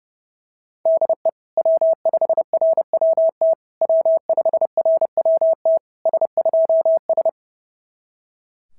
Morse code, heard as a keyed tone, about 28 wpm, 650 Hz. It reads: DE W5RWT W5RWT S2S
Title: SOTA QSO Practice 30wpm